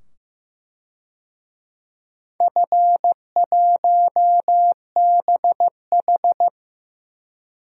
Morse code, heard as a keyed tone, about 15 wpm, 700 Hz. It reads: F1BH
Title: DXCC-100 Call Signs - 4 Characters - 15wpm